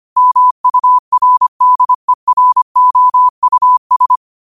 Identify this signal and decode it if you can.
Morse code, heard as a keyed tone, about 25 wpm, 1000 Hz. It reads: MURDEROUS
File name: 25wpm